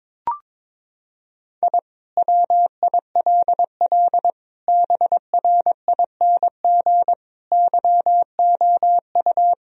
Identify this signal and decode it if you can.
Morse code, heard as a keyed tone, about 22 wpm, 700 Hz. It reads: I WILL BRING YOU
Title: Sentences from Top 1000 Words - Review Part 3 - 22wpm